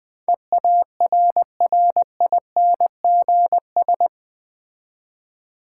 Morse code, heard as a keyed tone, about 20 wpm, 700 Hz. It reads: EARRINGS
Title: Difficult Words with Hints - 20wpm